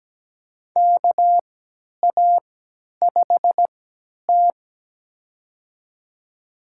Morse code, heard as a keyed tone, about 17 wpm, spaced as for 10 wpm, 700 Hz. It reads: KA5T
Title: USA Call Signs - 4 Characters - 3x Character Spacing - 17wpm